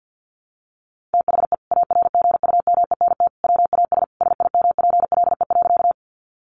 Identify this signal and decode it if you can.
Morse code, heard as a keyed tone, about 48 wpm, 700 Hz. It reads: THE ARGUMENT WAS SIMPLE.
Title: Sentences from Top 1400 Words 50wpm